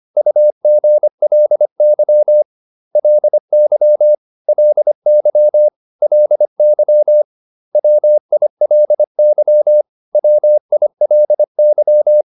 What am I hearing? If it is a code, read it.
Morse code, heard as a keyed tone, about 25 wpm, 600 Hz. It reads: UGLY LY LY LY WILY WILY